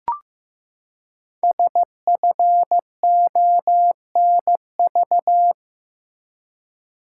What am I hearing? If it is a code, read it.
Morse code, heard as a keyed tone, about 15 wpm, 700 Hz. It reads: SFONV
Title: Code Groups - 5 Letters - 15wpm